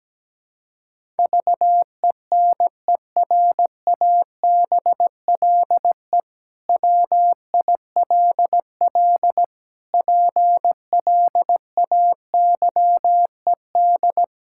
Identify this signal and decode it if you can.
Morse code, heard as a keyed tone, about 17 wpm, 700 Hz. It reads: VENERABLE WILL PLAYED
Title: Pangram Challenge v2 - 17wpm